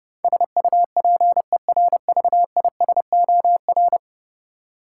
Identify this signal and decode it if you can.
Morse code, heard as a keyed tone, about 30 wpm, 700 Hz. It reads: SUPERVISOR
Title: Practice QSO - Exchange 3 - More Info - 30wpm